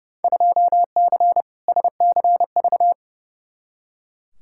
Morse code, heard as a keyed tone, about 32 wpm, 700 Hz. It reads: 2C SCV
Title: Field Day Exchanges Encore - 30wpm